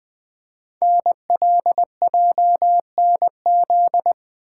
Morse code, heard as a keyed tone, about 20 wpm, 700 Hz. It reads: NLJNZ
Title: Code Groups - 5 Letters - 20wpm